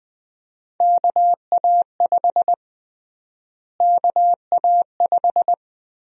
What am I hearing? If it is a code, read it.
Morse code, heard as a keyed tone, about 20 wpm, 700 Hz. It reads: KA5 KA5
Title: USA Call Sign Prefixes - Repeated 2x - 20wpm